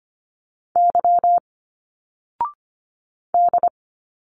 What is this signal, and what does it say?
Morse code, heard as a keyed tone, about 25 wpm, 700 Hz. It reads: Y D
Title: Single Letters 25wpm